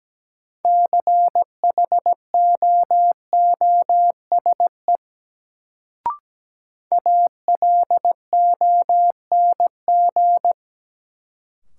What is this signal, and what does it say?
Morse code, heard as a keyed tone, about 17 wpm, 700 Hz. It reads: CHOOSE ALONG
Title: Sentences from Top 700 Words - New Word Drill - 17wpm